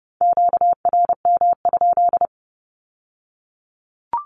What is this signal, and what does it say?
Morse code, heard as a keyed tone, about 30 wpm, 700 Hz. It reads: QRM?
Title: Q-Codes with Spoken Meaning - 30wpm